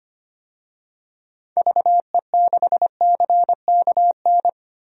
Morse code, heard as a keyed tone, about 25 wpm, 700 Hz. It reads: VE6CKN
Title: DXCC-100 Call Signs - 6 Characters - Repeated 3x - 25wpm